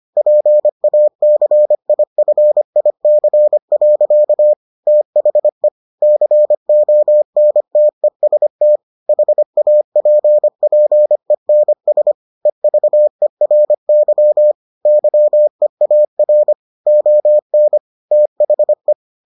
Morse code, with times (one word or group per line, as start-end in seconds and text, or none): 0.2-4.5 PACIFIC.
4.9-5.7 THE
6.0-8.8 CONTEST
9.1-12.1 HAPPENS
12.5-14.5 EVERY
14.9-16.5 YEAR
16.9-17.8 ON
18.1-18.9 THE